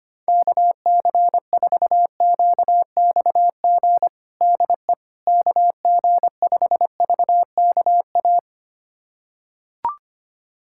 Morse code, heard as a keyed tone, about 25 wpm, 700 Hz. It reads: KC4QXG DE KG5VKA
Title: USA Callsign DE Callsign - 6 Characters - 25wpm